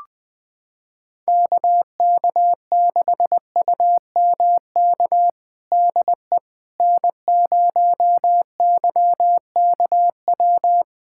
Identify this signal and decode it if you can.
Morse code, heard as a keyed tone, about 20 wpm, 700 Hz. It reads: KK6UMK DE N0YKW
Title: USA Callsign DE Callsign 20wpm